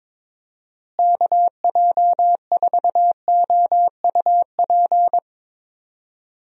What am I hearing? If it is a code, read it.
Morse code, heard as a keyed tone, about 22 wpm, 700 Hz. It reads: KJ4OUP
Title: USA Call Signs - 6 Characters - Repeated 3x - 22wpm